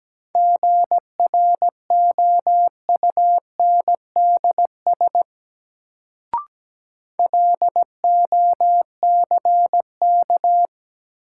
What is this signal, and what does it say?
Morse code, heard as a keyed tone, about 17 wpm, 700 Hz. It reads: GROUNDS LOCK
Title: Sentences from Top 1300 Words - New Word Drill - 17wpm